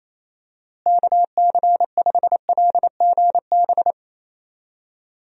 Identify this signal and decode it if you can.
Morse code, heard as a keyed tone, about 28 wpm, 700 Hz. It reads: KC5LGB
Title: USA Call Signs - 6 Characters - Repeated 3x - 28wpm